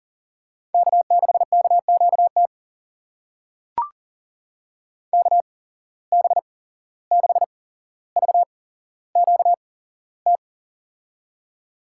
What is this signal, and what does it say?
Morse code, heard as a keyed tone, about 40 wpm, spaced as for 16 wpm, 700 Hz. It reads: K6KQT KB6VQT
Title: USA Call Signs - 8x Character Spacing - 40wpm